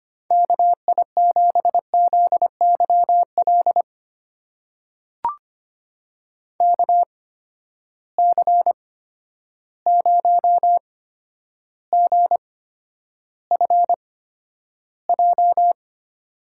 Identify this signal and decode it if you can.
Morse code, heard as a keyed tone, about 25 wpm, spaced as for 11 wpm, 700 Hz. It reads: KI7ZYL KC0GFJ
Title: USA Call Signs - 6 Characters - 8x Character Spacing - 25wpm